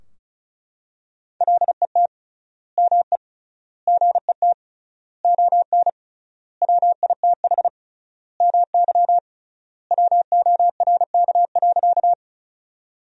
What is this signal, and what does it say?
Morse code, heard as a keyed tone, about 36 wpm, 700 Hz. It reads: LET ME GET ON WITH MY WORK.
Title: 3x Word-Spacing - Sentences from Top 300 Words - 35wpm